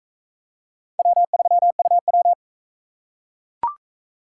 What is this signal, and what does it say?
Morse code, heard as a keyed tone, about 44 wpm, 700 Hz. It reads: W3UW